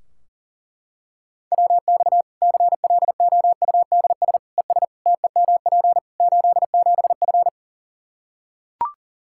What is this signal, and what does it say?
Morse code, heard as a keyed tone, about 40 wpm, 700 Hz. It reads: WX CLOUDS ES TEMP 87F
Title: Practice QSO - Exchange 2 - Info - 40wpm